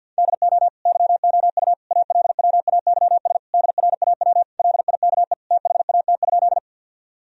Morse code, heard as a keyed tone, about 48 wpm, 700 Hz. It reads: DO YOU ALWAYS DRAW LIKE THAT?
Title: Sentences from Top 1000 Words - Review Part 1 - 50wpm